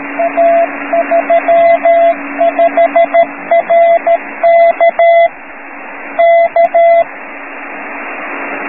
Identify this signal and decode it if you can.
Morse code, heard as a keyed tone, about 13 wpm, 680 Hz, with radio noise. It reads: A35RK K